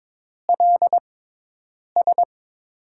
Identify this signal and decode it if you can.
Morse code, heard as a keyed tone, about 22 wpm, 700 Hz. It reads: L S
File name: Top 90 QSO Elements - 6x Character Spacing - 22wpm